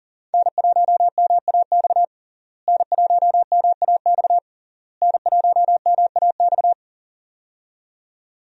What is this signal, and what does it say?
Morse code, heard as a keyed tone, about 37 wpm, 700 Hz. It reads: N1MAX N1MAX N1MAX
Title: USA Call Signs - 5 Characters - Repeated 3x - 40wpm